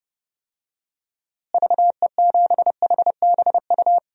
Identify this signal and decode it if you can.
Morse code, heard as a keyed tone, about 30 wpm, 700 Hz. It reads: VE7HBU
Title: DXCC-100 Call Signs - 6 Characters - Repeated 2x - 30wpm